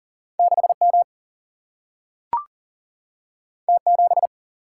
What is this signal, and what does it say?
Morse code, heard as a keyed tone, about 39 wpm, 700 Hz. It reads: 6 M T 7
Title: Single-Letter Single-Number 40wpm